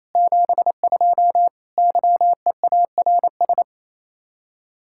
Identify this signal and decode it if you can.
Morse code, heard as a keyed tone, about 34 wpm, 700 Hz. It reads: 72 YEARS
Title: Speed-Racing Common QSO Phrases Encore - 28wpm